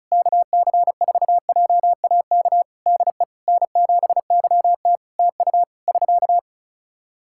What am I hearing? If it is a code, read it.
Morse code, heard as a keyed tone, about 34 wpm, 700 Hz. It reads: KC4JAK DE N7YT TU <SK>
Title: Practice QSO - Exchange 4 - Ending - 35wpm